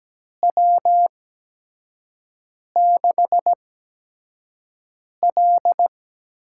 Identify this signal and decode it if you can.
Morse code, heard as a keyed tone, about 17 wpm, spaced as for 5 wpm, 700 Hz. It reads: W6L
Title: USA Call Signs - 4 Characters - 8x Character Spacing - 17wpm